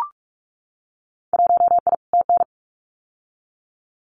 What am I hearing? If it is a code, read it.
Morse code, heard as a keyed tone, about 44 wpm, 700 Hz. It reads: 1I TN
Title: Winter Field Day Exchanges Encore 45wpm